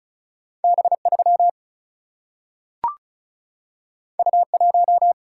Morse code, heard as a keyed tone, about 36 wpm, 700 Hz. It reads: B 3 U 1
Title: Single-Letter Single-Number 35wpm